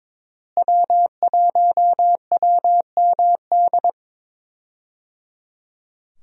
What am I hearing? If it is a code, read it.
Morse code, heard as a keyed tone, about 21 wpm, 700 Hz. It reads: W1WMD
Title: USA Call Signs - 5 Characters - Repeated 3x - 22wpm